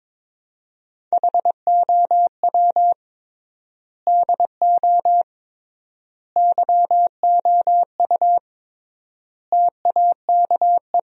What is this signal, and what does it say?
Morse code, heard as a keyed tone, about 22 wpm, 700 Hz. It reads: HOW DO YOU TAKE
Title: 3x Word-Spacing - Sentences from Top 200 Words - 22wpm